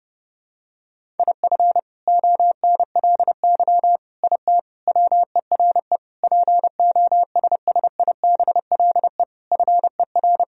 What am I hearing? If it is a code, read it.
Morse code, heard as a keyed tone, about 30 wpm, 700 Hz. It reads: IF ONLY IT WERE POSSIBLE FER
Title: Sentences from Top 500 Words and Abbreviations 30wpm